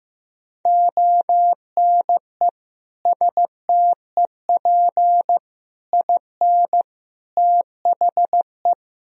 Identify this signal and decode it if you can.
Morse code, heard as a keyed tone, about 16 wpm, 700 Hz. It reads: ONE STEP IN THE
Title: Sentences from Top 1000 Words - Review Part 4 - 15wpm